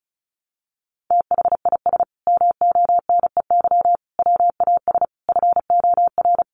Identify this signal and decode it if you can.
Morse code, heard as a keyed tone, about 34 wpm, 700 Hz. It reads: THIS MONEY WAS FOR
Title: Sentences from Top 400 Words 35wpm